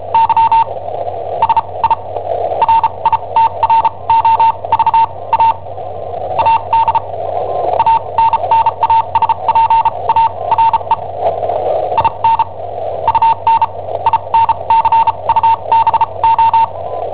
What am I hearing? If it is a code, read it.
Morse code, heard as a keyed tone, about 34 wpm, spaced as for 26 wpm, 910 Hz, with radio noise. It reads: Y SI RITROVA AD ANNASPARE IN UN INCUBO
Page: Cassetta esercizi preparata da I7OHP per IW5ELP